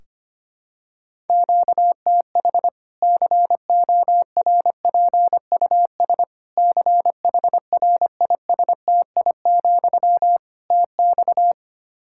Morse code, heard as a keyed tone, about 25 wpm, 700 Hz. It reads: QTH CORPUS CHRISTI, TX
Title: Practice QSO - Exchange 1 - Beginning - 25wpm